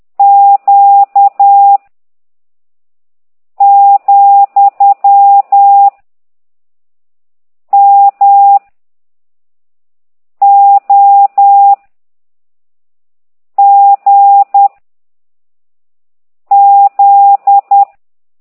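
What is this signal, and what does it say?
Morse code, heard as a keyed tone, about 10 wpm, 800 Hz. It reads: Q , M O G Z
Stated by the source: Morse code lesson 5 exercise 10 by CT2IDW, containing only characters M, O, G, Z, Q and comma.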